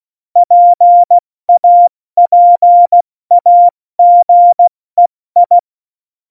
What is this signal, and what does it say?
Morse code, heard as a keyed tone, about 16 wpm, 700 Hz. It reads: PAPAGEI
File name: Morsekurs / Witze / Witz 20. November 2018, 70 F80